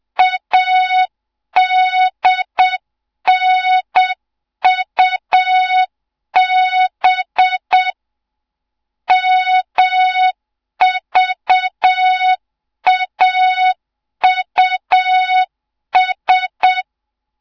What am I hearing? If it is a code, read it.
Morse code, heard as a keyed tone, about 7 wpm, 3700 Hz. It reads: ADNUB MVAUS
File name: Morze-HA2MN / hang / 07 GYAKORLAT